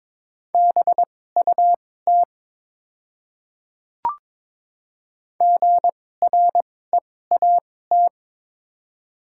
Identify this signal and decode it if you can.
Morse code, heard as a keyed tone, about 22 wpm, spaced as for 16 wpm, 700 Hz. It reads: BUT GREAT
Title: 2x Character Spacing - Top 300 Words - 22wpm